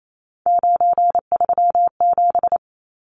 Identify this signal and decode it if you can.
Morse code, heard as a keyed tone, about 28 wpm, 700 Hz. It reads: 937
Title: Numbers - 3-Digit 28wpm